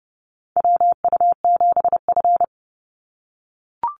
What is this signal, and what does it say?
Morse code, heard as a keyed tone, about 30 wpm, 700 Hz. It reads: WU7F
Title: Call Signs - 30wpm